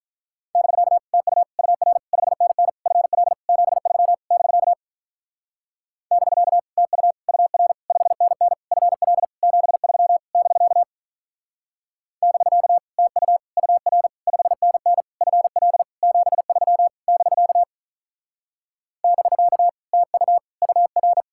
Morse code, heard as a keyed tone, about 45 wpm, 700 Hz. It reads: <BK> TU UR 5NN FL 73 <BK> <BK> TU UR 5NN FL 73 <BK> <BK> TU UR 5NN FL 73 <BK> <BK> TU UR